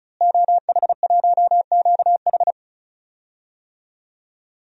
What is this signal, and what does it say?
Morse code, heard as a keyed tone, about 36 wpm, 700 Hz. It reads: OH1QH
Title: DXCC-100 Call Signs - 5 Characters - Repeated 3x - 35wpm